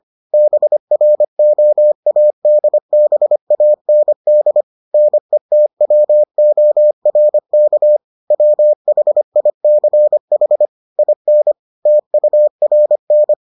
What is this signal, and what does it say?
Morse code, heard as a keyed tone, about 25 wpm, 600 Hz. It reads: BROADBAND NETWORK WHICH IN TURN